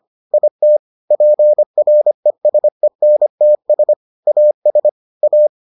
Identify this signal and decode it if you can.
Morse code, heard as a keyed tone, about 25 wpm, 600 Hz. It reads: IT PRESENTS AS A